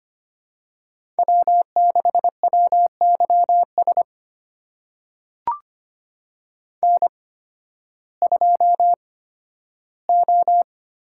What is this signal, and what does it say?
Morse code, heard as a keyed tone, about 25 wpm, spaced as for 13 wpm, 700 Hz. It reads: W6WYS N2O
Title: USA Call Signs - 8x Character Spacing - 25wpm